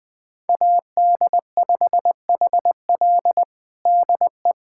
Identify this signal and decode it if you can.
Morse code, heard as a keyed tone, about 20 wpm, 700 Hz. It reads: AD5HL DE
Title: USA Callsign DE Callsign - 5 Characters - 20wpm